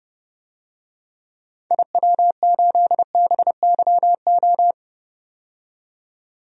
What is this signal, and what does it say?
Morse code, heard as a keyed tone, about 30 wpm, 700 Hz. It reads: IW8BYO
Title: DXCC-100 Call Signs - 6 Characters - 30wpm